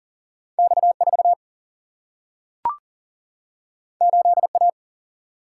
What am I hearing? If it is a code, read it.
Morse code, heard as a keyed tone, about 40 wpm, 700 Hz. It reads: X4 8A
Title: Single-Letter Single-Number 40wpm